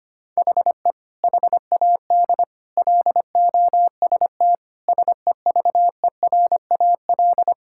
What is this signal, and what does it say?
Morse code, heard as a keyed tone, about 25 wpm, 700 Hz. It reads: HE HAD LOST SEVERAL
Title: Sentences from Top 1600 Words 25wpm